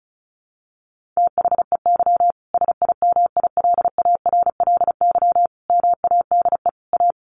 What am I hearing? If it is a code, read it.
Morse code, heard as a keyed tone, about 34 wpm, 700 Hz. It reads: THEY SIMILARLY MADE A
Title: Sentences from Top 1400 Words 35wpm